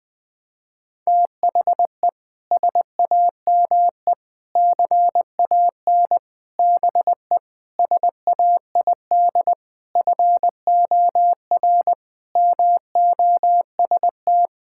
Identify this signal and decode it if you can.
Morse code, heard as a keyed tone, about 20 wpm, 700 Hz. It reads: THE SAME CAN BE SAID FOR MOST
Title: Sentences with 6-Word Phrases - 20wpm